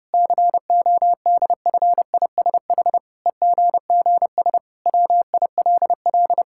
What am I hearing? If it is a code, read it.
Morse code, heard as a keyed tone, about 32 wpm, 700 Hz. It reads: CODFISH EGGS WILL
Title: Pangram Challenge v2 - 30wpm